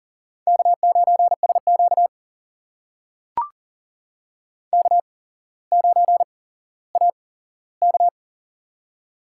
Morse code, heard as a keyed tone, about 40 wpm, spaced as for 17 wpm, 700 Hz. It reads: K9SQ K9AK
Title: USA Call Signs - 4 Characters - 8x Character Spacing - 40wpm